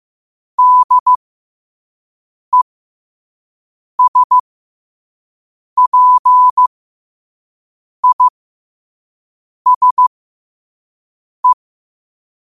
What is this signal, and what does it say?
Morse code, heard as a keyed tone, about 15 wpm, spaced as for 5 wpm, 1000 Hz. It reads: DESPISE
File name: qotdfiles / 5wpm